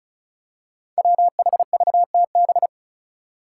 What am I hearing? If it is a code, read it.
Morse code, heard as a keyed tone, about 34 wpm, 700 Hz. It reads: WHVTB
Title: Five Challenging Letter-Number Combinations 35wpm